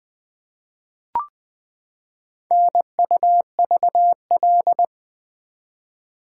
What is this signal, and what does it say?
Morse code, heard as a keyed tone, about 20 wpm, 700 Hz. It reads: NUVL
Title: Four Letter-Number Combinations - 20wpm